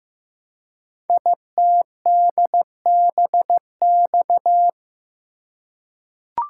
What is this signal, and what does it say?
Morse code, heard as a keyed tone, about 15 wpm, 700 Hz. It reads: ITDBX
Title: Code Groups - 5 Letters - 15wpm